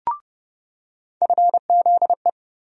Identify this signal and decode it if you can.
Morse code, heard as a keyed tone, about 31 wpm, 700 Hz. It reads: FZE
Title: USA Call Sign Postfixes - 3 Letters 30wpm